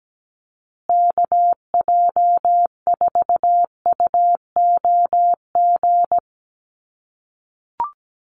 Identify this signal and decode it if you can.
Morse code, heard as a keyed tone, about 17 wpm, 700 Hz. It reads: KJ4UOG
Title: USA Call Signs Repeated 3x 17wpm